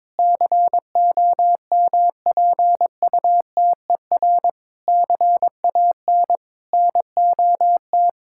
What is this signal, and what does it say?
Morse code, heard as a keyed tone, about 22 wpm, 700 Hz. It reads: COMPUTER CAN NOT
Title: Sentences from Top 1000 Words - Review Part 6 - 22wpm